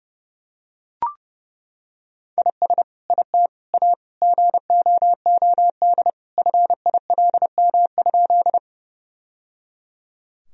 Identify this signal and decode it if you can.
Morse code, heard as a keyed tone, about 30 wpm, 700 Hz. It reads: IS IT A GOOD FILM?
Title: Sentences from Top 1000 Words - Review Part 3 - 30wpm